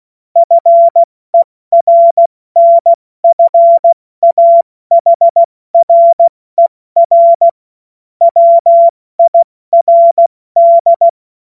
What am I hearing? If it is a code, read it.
Morse code, heard as a keyed tone, about 16 wpm, spaced as for 14 wpm, 680 Hz. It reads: FERNFAHRER WIRD
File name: Morsekurs / Witze / Witz 27. Januar 2026 70 F80